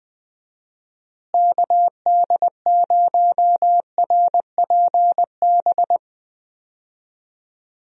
Morse code, heard as a keyed tone, about 21 wpm, 700 Hz. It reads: KD0RPB
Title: USA Call Signs - 6 Characters - Repeated 3x - 20wpm